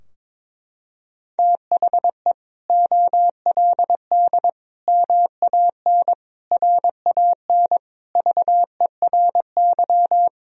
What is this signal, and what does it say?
Morse code, heard as a keyed tone, about 22 wpm, 700 Hz. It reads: THE OLD MAN RAN VERY
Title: Sentences from Top 1000 Words - Review Part 3 - 22wpm